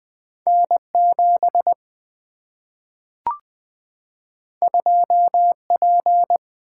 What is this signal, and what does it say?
Morse code, heard as a keyed tone, about 20 wpm, 700 Hz. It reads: N7 2P
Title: Single-Letter Single-Number 20wpm